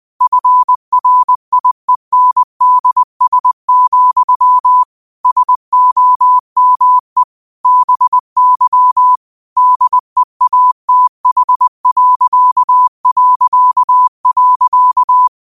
Morse code, heard as a keyed tone, about 20 wpm, 1000 Hz. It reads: FRIENDS, SOME BY DEATH...
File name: qotdfiles / 20wpm